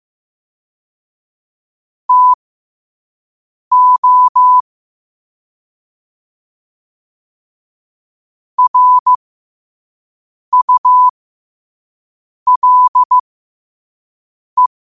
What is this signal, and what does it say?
Morse code, heard as a keyed tone, about 15 wpm, spaced as for 5 wpm, 1000 Hz. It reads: TO RULE
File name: qotdfiles / 5wpm